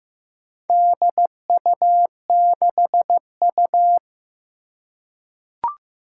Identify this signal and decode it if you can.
Morse code, heard as a keyed tone, about 15 wpm, 700 Hz. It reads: DU6U
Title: Four Challenging Letter-Number Combinations 15wpm